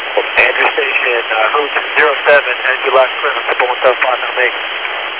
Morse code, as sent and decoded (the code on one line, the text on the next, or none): none